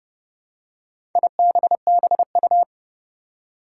Morse code, heard as a keyed tone, about 32 wpm, 700 Hz. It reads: IBBU
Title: Four Challenging Letter-Number Combinations 30wpm